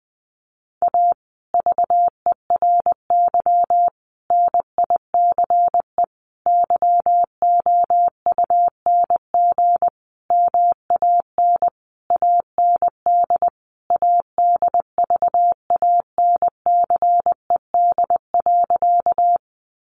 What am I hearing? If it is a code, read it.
Morse code, heard as a keyed tone, about 20 wpm, 700 Hz. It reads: A VERY NICE YOUNG MAN AND ADVANCED.
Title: Sentences from Top 1200 Words 20wpm